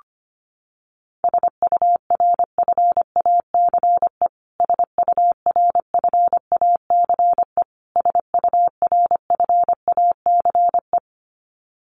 Morse code, heard as a keyed tone, about 25 wpm, 700 Hz. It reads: SURFACE SURFACE SURFACE
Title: Top 500 Words Repeated 25wpm